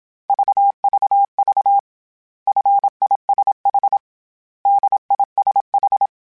Morse code, heard as a keyed tone, about 27 wpm, 800 Hz. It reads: VVV FISH DISH